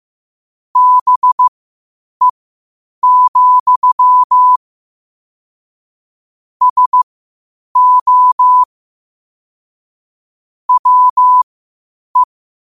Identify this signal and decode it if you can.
Morse code, heard as a keyed tone, about 15 wpm, spaced as for 8 wpm, 1000 Hz. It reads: BE, SO WE
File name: qotdfiles / 8wpm